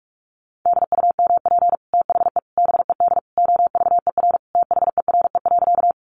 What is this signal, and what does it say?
Morse code, heard as a keyed tone, about 44 wpm, 700 Hz. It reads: DUMP THE BED OVER THERE.
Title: Sentences from Top 1300 Words 45wpm